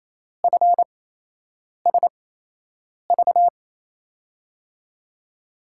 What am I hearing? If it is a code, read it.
Morse code, heard as a keyed tone, about 28 wpm, spaced as for 8 wpm, 700 Hz. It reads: FSV
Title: USA Call Signs - 5 Characters - 8x Character Spacing - 28wpm